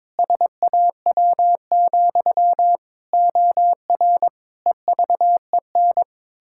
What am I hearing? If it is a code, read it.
Morse code, heard as a keyed tone, about 22 wpm, 700 Hz. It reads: SAW, OR EVEN
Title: Pangram Challenge v2 - 22wpm